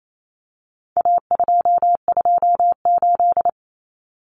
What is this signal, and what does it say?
Morse code, heard as a keyed tone, about 28 wpm, 700 Hz. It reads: A228
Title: Common Cut Numbers - 4-Digit 28wpm